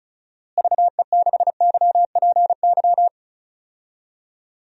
Morse code, heard as a keyed tone, about 36 wpm, 700 Hz. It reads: VE6YPY
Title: DXCC-100 Call Signs - 6 Characters - Repeated 2x - 35wpm